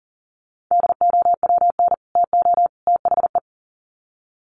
Morse code, heard as a keyed tone, about 40 wpm, 700 Hz. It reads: DOWN TO THE
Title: Sets of 3 Words 40wpm